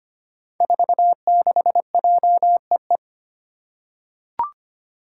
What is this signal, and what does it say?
Morse code, heard as a keyed tone, about 25 wpm, 700 Hz. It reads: 46JEE
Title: Five Challenging Letter-Number Combinations 25wpm